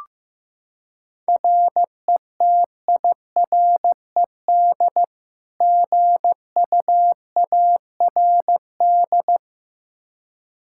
Morse code, heard as a keyed tone, about 15 wpm, 700 Hz. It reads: RETIRED GUARD
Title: Practice QSO - Exchange 3 - More Info - 15wpm